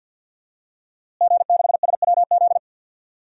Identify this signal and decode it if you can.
Morse code, heard as a keyed tone, about 51 wpm, 700 Hz. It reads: G6SWZ
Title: DXCC-100 Call Signs - 5 Characters - Repeated 2x - 50wpm